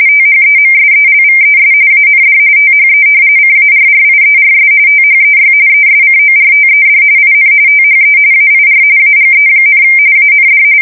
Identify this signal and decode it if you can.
RTTY, about 45.45 baud. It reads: THEMSELVES AND THE CARRIERS THEY ESCORTED IN THE AIR AGE, AND
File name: fsk170 / battleshipnj